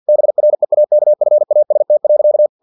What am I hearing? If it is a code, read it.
Morse code, heard as a keyed tone, about 50 wpm, 600 Hz. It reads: BREAKFAST.
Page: Never 49wpm